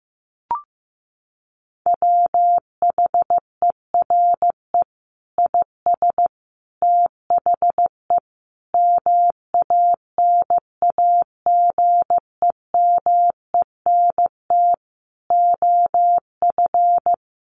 Morse code, heard as a keyed tone, about 15 wpm, 700 Hz. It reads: WHERE IS THE MANAGEMENT OF
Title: Sentences from Top 1200 Words 15wpm